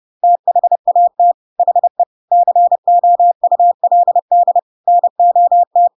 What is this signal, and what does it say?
Morse code, heard as a keyed tone, about 30 wpm, 700 Hz. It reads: THAT HE COULD NOT
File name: Pinocchio / Pinocchio ch3 30 30 ik1hge